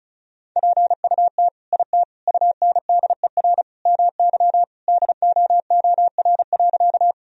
Morse code, heard as a keyed tone, about 34 wpm, 700 Hz. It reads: PUT IT UNDER MY DOOR.
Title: Sentences from Top 1000 Words - Review Part 1 - 35wpm